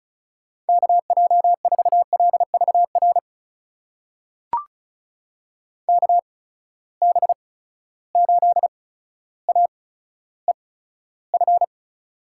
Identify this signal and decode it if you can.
Morse code, heard as a keyed tone, about 36 wpm, spaced as for 15 wpm, 700 Hz. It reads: KJ4LVR KB8AEF
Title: USA Call Signs - 6 Characters - 8x Character Spacing - 35wpm